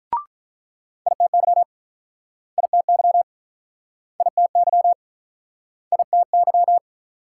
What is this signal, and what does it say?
Morse code, heard as a keyed tone, about 48 wpm, 700 Hz. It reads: ITY ITY ITY ITY